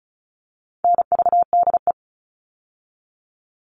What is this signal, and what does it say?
Morse code, heard as a keyed tone, about 34 wpm, 700 Hz. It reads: NVDE
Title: All Cut Numbers - 4-Digit 35wpm